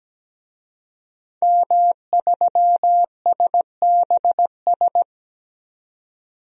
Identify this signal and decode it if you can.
Morse code, heard as a keyed tone, about 17 wpm, 700 Hz. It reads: M3SBS
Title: DXCC-100 Call Signs - 5 Characters - 17wpm